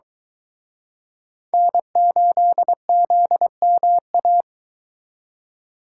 Morse code, heard as a keyed tone, about 22 wpm, 700 Hz. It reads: N8ZMA